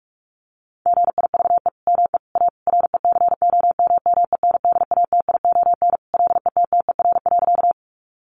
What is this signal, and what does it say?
Morse code, heard as a keyed tone, about 44 wpm, 700 Hz. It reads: GIVE ME A RECOMMENDATION LETTER.
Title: Sentences from Top 1400 Words 45wpm